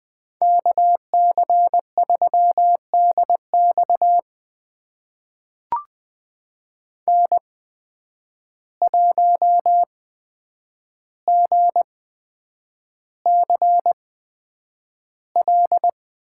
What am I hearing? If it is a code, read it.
Morse code, heard as a keyed tone, about 20 wpm, spaced as for 9 wpm, 700 Hz. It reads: KC3DX N1GCL
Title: USA Call Signs - 5 Characters - 8x Character Spacing - 20wpm